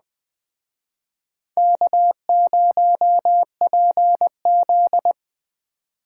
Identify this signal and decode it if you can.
Morse code, heard as a keyed tone, about 20 wpm, 700 Hz. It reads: K0PZ